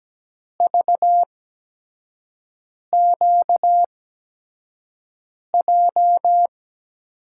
Morse code, heard as a keyed tone, about 17 wpm, spaced as for 5 wpm, 700 Hz. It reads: VQJ
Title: USA Call Signs - 6 Characters - 8x Character Spacing - 17wpm